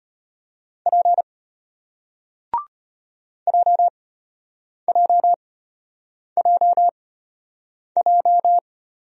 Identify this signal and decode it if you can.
Morse code, heard as a keyed tone, about 34 wpm, spaced as for 8 wpm, 700 Hz. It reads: P JJJJ